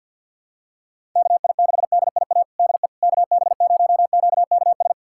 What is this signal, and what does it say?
Morse code, heard as a keyed tone, about 49 wpm, 700 Hz. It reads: KI6DIA DE KB0QKS
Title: USA Callsign DE Callsign 50wpm